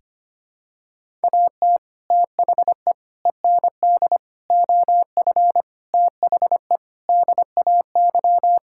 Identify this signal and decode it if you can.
Morse code, heard as a keyed tone, about 25 wpm, 700 Hz. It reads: AT THE END OF THE DAY
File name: practice / 25wpm / Sentences with 4-Word Phrases - 25wpm